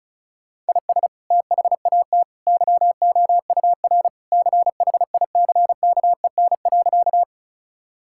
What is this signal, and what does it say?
Morse code, heard as a keyed tone, about 34 wpm, 700 Hz. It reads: IS THAT YOUR CHICKEN.
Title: Sentences from Top 1000 Words - Review Part 5 - 35wpm